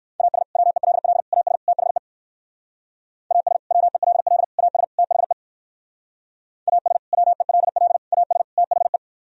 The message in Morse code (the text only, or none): AS WELL AS THE AS WELL AS THE AS WELL AS THE